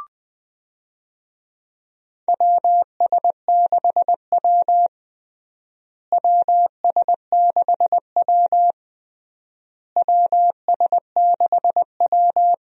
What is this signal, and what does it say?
Morse code, heard as a keyed tone, about 20 wpm, 700 Hz. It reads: WS6W WS6W WS6W
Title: USA Call Signs - 4 Characters - Repeated 3x - 20wpm